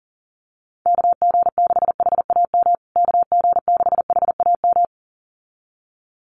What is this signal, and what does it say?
Morse code, heard as a keyed tone, about 40 wpm, 700 Hz. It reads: KG6HAM KG6HAM
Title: USA Call Signs Repeated 2x 40wpm